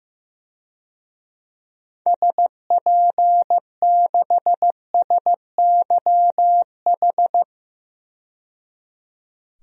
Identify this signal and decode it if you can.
Morse code, heard as a keyed tone, about 15 wpm, 700 Hz. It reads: SP6SYH
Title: DXCC-100 Call Signs - 6 Characters - 15wpm